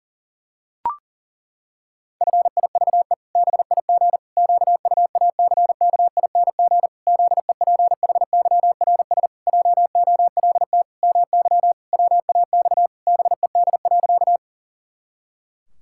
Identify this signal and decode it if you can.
Morse code, heard as a keyed tone, about 37 wpm, 700 Hz. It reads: FIVE BIG QUACKING ZEPHYRS JOLT MY WAX BED.
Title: Pangram Challenge v2 - 40wpm